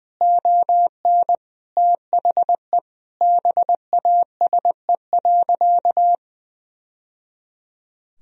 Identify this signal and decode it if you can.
Morse code, heard as a keyed tone, about 21 wpm, 700 Hz. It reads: ON THE BASE.
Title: Sentences from Top 1000 Words - Review Part 1 - 20wpm